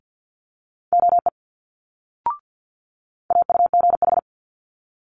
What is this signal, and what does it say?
Morse code, heard as a keyed tone, about 50 wpm, 700 Hz. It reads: OE AUGH
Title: Multi-letter Phonemes Quick Fire 50wpm